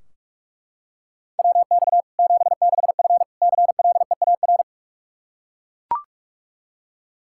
Morse code, heard as a keyed tone, about 44 wpm, 700 Hz. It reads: WX 76F CLEAR
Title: K1USN SST QSO Practice 45wpm